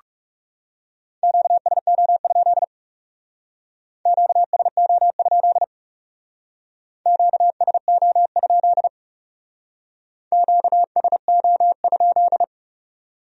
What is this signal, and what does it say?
Morse code, heard as a keyed tone, about 38 wpm, 700 Hz. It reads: QSO? QSO? QSO? QSO?